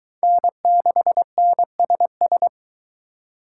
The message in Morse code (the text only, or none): N6NSS